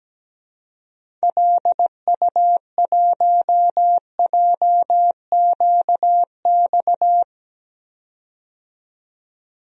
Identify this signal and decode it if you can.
Morse code, heard as a keyed tone, about 17 wpm, 700 Hz. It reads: LU1JQX
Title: DXCC-100 Call Signs - 6 Characters - 17wpm